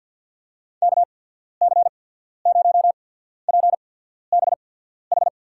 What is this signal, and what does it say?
Morse code, heard as a keyed tone, about 53 wpm, spaced as for 14 wpm, 700 Hz. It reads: KC0PBH
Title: USA Call Signs - 6 Characters - 8x Character Spacing - 50wpm